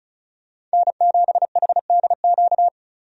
Morse code, heard as a keyed tone, about 36 wpm, 700 Hz. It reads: N7HDQ
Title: USA Call Signs - 5 Characters - 35wpm